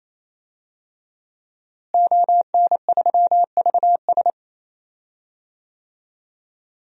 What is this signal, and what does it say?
Morse code, heard as a keyed tone, about 28 wpm, 700 Hz. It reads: ON3VS
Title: DXCC-100 Call Signs - 5 Characters - Repeated 3x - 28wpm